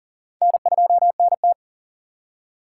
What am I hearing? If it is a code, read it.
Morse code, heard as a keyed tone, about 40 wpm, 700 Hz. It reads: N2NT
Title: USA Call Signs - 4x Character Spacing - 40wpm